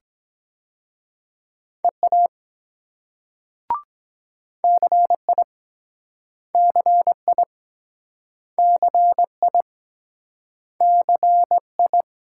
Morse code, heard as a keyed tone, about 23 wpm, 700 Hz. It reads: EA CI CI CI CI